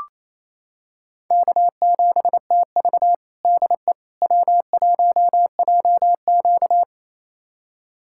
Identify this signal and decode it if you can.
Morse code, heard as a keyed tone, about 28 wpm, 700 Hz. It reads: K7TV DE W1JQ
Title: USA Callsign DE Callsign - 4 Characters - 28wpm